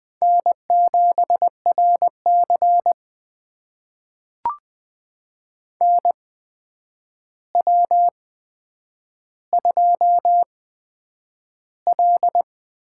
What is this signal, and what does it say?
Morse code, heard as a keyed tone, about 20 wpm, spaced as for 9 wpm, 700 Hz. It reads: N7RC NW2L
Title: USA Call Signs - 4 Characters - 8x Character Spacing - 20wpm